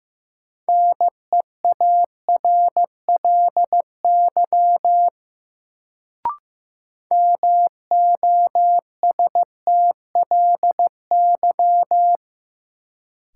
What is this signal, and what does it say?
Morse code, heard as a keyed tone, about 15 wpm, 700 Hz. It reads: NEARLY MOSTLY
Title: Sentences from Top 1300 Words - New Word Drill Alt - 15wpm